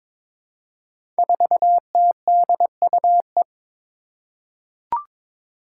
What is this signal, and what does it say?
Morse code, heard as a keyed tone, about 22 wpm, 700 Hz. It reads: 4TDUE
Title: Five Challenging Letter-Number Combinations 22wpm